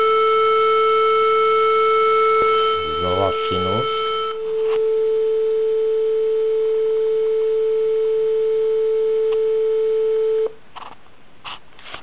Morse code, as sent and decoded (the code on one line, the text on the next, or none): none